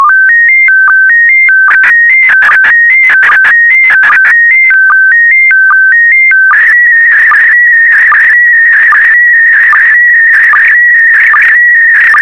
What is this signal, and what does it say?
SSTV (slow-scan television), PD 160.
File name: sstv / PD160-VK4MSL